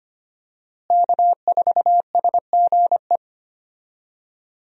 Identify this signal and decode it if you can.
Morse code, heard as a keyed tone, about 25 wpm, 700 Hz. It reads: K4SGE
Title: USA Call Signs - 5 Characters - Repeated 3x - 25wpm